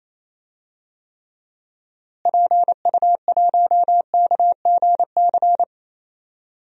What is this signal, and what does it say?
Morse code, heard as a keyed tone, about 27 wpm, 700 Hz. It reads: PU1KGC
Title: DXCC-100 Call Signs - 6 Characters - Repeated 3x - 28wpm